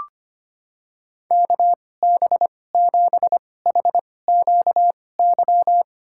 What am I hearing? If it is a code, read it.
Morse code, heard as a keyed tone, about 25 wpm, spaced as for 18 wpm, 690 Hz. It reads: KB7HQY
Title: USA Call Signs - 6 Characters - 2x Character Spacing - 25wpm